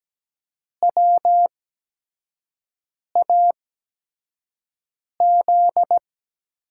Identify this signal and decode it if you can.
Morse code, heard as a keyed tone, about 17 wpm, spaced as for 5 wpm, 700 Hz. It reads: WAZ
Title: USA Call Signs - 8x Character Spacing - 17wpm